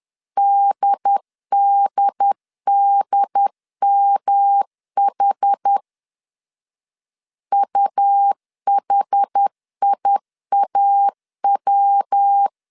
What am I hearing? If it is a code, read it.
Morse code, heard as a keyed tone, about 10 wpm, 800 Hz. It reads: DDDMH UHIAW
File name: CW / DET / 1 3 1500C S C 10wpm 3335